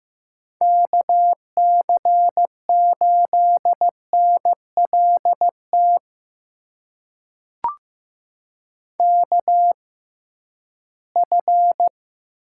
Morse code, heard as a keyed tone, about 15 wpm, spaced as for 11 wpm, 700 Hz. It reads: KC8NLT KF
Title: USA Call Signs - 6x Character Spacing - 15wpm